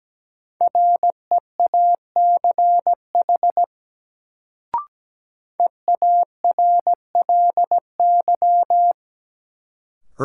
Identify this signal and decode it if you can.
Morse code, heard as a keyed tone, about 17 wpm, 700 Hz. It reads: REACH EARLY
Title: Sentences from Top 400 Words - New Word Drill - 17wpm